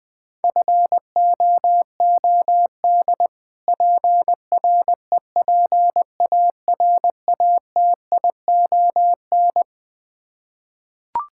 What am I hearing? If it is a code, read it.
Morse code, heard as a keyed tone, about 20 wpm, 700 Hz. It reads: FOOD PREPARATION
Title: K1USN SST QSO Practice 20wpm